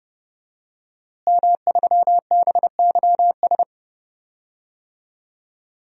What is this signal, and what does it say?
Morse code, heard as a keyed tone, about 30 wpm, 700 Hz. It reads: M3BYS
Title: DXCC-100 Call Signs - 5 Characters - Repeated 2x - 30wpm